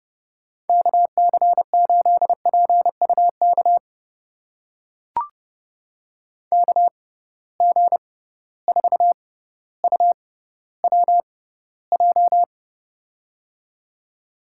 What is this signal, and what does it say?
Morse code, heard as a keyed tone, about 30 wpm, spaced as for 15 wpm, 700 Hz. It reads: KC8PUK KG4UWJ
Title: USA Call Signs - 6 Characters - 6x Character Spacing - 30wpm